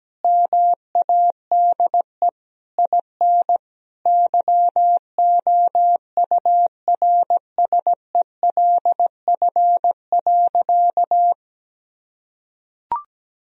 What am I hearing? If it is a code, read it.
Morse code, heard as a keyed tone, about 17 wpm, 700 Hz. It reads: MADE IN YOURSELF.
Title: Sentences from Top 1000 Words - Review Part 1 - 17wpm